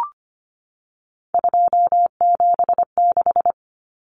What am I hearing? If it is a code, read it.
Morse code, heard as a keyed tone, about 25 wpm, 700 Hz. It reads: 276
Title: Numbers - 3-Digit 25wpm